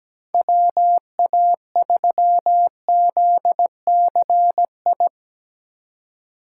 Morse code, heard as a keyed tone, about 17 wpm, 700 Hz. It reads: WA3ZCI
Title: USA Call Signs - 6 Characters - 3x Character Spacing - 17wpm